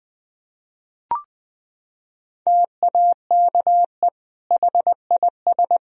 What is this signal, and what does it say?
Morse code, heard as a keyed tone, about 20 wpm, 700 Hz. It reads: TAKE HIS
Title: Sentences from Top 1000 Words - Review Part 3 - 20wpm